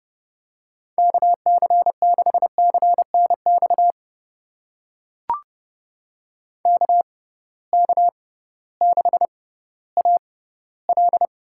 Morse code, heard as a keyed tone, about 30 wpm, spaced as for 16 wpm, 700 Hz. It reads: KC6CNX KK6AL
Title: USA Call Signs - 6 Characters - 6x Character Spacing - 30wpm